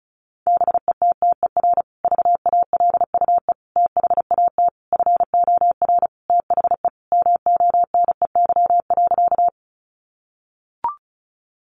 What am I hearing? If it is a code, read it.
Morse code, heard as a keyed tone, about 34 wpm, 700 Hz. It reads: BETTER VALUE THAT FOR THE MONEY.
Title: Sentences from Top 600 Words 35wpm